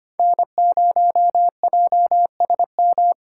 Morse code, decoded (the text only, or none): N0JSM